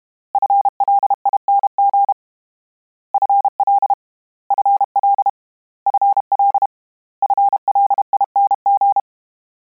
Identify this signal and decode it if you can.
Morse code, heard as a keyed tone, about 32 wpm, 800 Hz. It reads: FLING FL FL FL FLING
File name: Combinations / Combinations-mp3s / Combinations@30